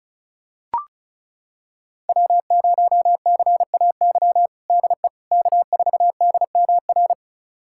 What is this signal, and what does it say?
Morse code, heard as a keyed tone, about 36 wpm, 700 Hz. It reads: W0CAY DE K4DMR
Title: USA Callsign DE Callsign - 5 Characters - 35wpm